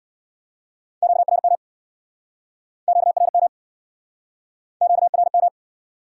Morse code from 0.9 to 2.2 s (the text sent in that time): QRM